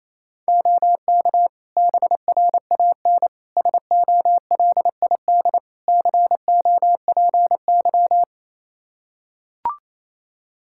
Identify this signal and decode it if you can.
Morse code, heard as a keyed tone, about 28 wpm, 700 Hz. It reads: OK BRAN SOLID COPY
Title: Practice QSO - Exchange 3 - More Info - 28wpm